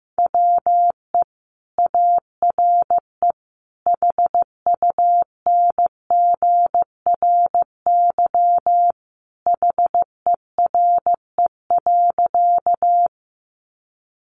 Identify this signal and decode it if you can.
Morse code, heard as a keyed tone, about 15 wpm, 700 Hz. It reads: WE ARE HUNGRY HERE.
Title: Sentences from Top 1400 Words 15wpm